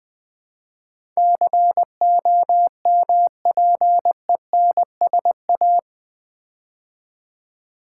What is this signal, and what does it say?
Morse code, heard as a keyed tone, about 20 wpm, 700 Hz. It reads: COMPENSA
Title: Build A Word - 4 Letters At A Time - 12-Letter Words - 20wpm